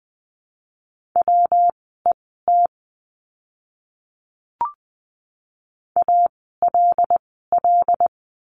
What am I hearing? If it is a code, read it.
Morse code, heard as a keyed tone, about 20 wpm, spaced as for 14 wpm, 700 Hz. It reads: WET ALL
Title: 115 Lesson 19f - Letter w - Words 20wpm